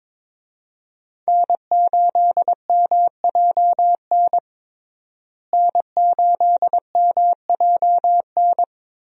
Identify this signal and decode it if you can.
Morse code, heard as a keyed tone, about 22 wpm, 700 Hz. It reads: N8MJN N8MJN
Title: USA Call Signs - 5 Characters - Repeated 3x - 22wpm